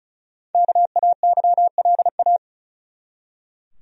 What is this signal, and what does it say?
Morse code, heard as a keyed tone, about 36 wpm, 700 Hz. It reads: KAYLA
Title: USA Common Names - Max 5 Letters - 35wpm